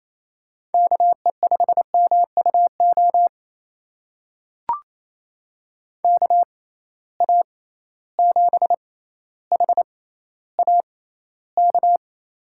Morse code, heard as a keyed tone, about 28 wpm, spaced as for 14 wpm, 700 Hz. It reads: KE5MUO KA7HAK
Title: USA Call Signs - 6 Characters - 6x Character Spacing - 28wpm